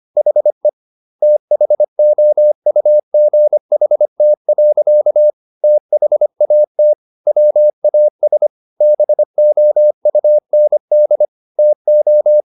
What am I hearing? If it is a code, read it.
Morse code, heard as a keyed tone, about 25 wpm, 600 Hz. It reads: HE THOUGHT. THAT WAS BOUND TO